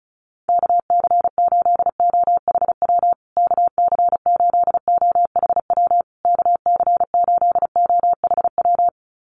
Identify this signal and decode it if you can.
Morse code, heard as a keyed tone, about 34 wpm, 700 Hz. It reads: KC8OHW KC8OHW KC8OHW
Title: USA Call Signs Repeated 3x 35wpm